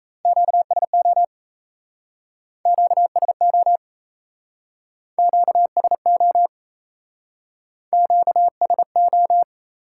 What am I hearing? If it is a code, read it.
Morse code, heard as a keyed tone, about 37 wpm, 700 Hz. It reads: QSO QSO QSO QSO